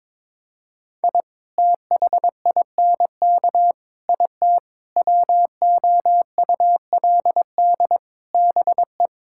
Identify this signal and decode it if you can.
Morse code, heard as a keyed tone, about 22 wpm, 700 Hz. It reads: I THINK IT WOULD BE
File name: practice / 22wpm / Build A Phrase - 5-Word Phrases - 22wpm